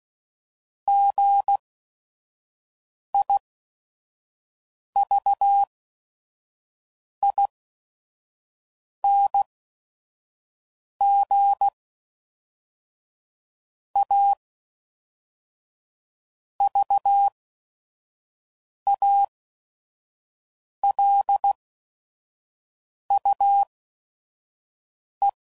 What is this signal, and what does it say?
Morse code, heard as a keyed tone, about 16 wpm, spaced as for 5 wpm, 800 Hz. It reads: GIVING A VALUE
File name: CW Code Groups @5WPM / Technical Subjects / XL